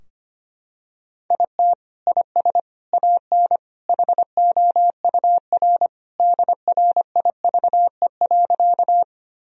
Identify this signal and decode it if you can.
Morse code, heard as a keyed tone, about 25 wpm, 700 Hz. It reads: IT IS AN HOUR DRIVE.
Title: Sentences from Top 1000 Words - Review Part 2 - 25wpm